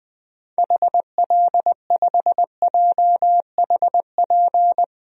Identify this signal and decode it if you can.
Morse code, heard as a keyed tone, about 20 wpm, 700 Hz. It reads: HL5JHP
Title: DXCC-100 Call Signs - 6 Characters - Repeated 2x - 20wpm